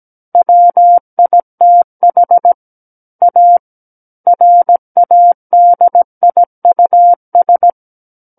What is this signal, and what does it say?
Morse code, heard as a keyed tone, about 17 wpm, 700 Hz. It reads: WITH A RADIUS
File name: Book One / 08-Chapter 8 - 17 WPM